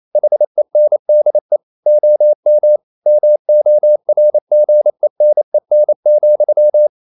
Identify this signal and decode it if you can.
Morse code, heard as a keyed tone, about 28 wpm, 600 Hz. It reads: HENDE OM MORGENEN,
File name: Ferie QRQ 28wpm